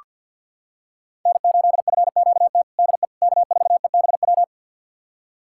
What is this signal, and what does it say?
Morse code, heard as a keyed tone, about 46 wpm, 700 Hz. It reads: N8FQT DE K4EBW
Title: USA Callsign DE Callsign - 5 Characters - 50wpm